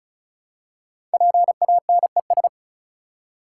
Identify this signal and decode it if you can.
Morse code, heard as a keyed tone, about 36 wpm, 700 Hz. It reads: PANES
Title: Code Groups - 5 Letters - 35wpm